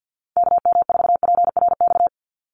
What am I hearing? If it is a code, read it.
Morse code, heard as a keyed tone, about 50 wpm, 700 Hz. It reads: KM4PRX
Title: USA Call Signs 50wpm